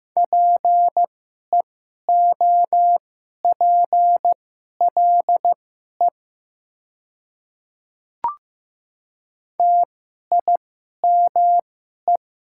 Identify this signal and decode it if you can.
Morse code, heard as a keyed tone, about 15 wpm, spaced as for 11 wpm, 700 Hz. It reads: PEOPLE TIME
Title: 2x Character Spacing - Top 200 Words - 15wpm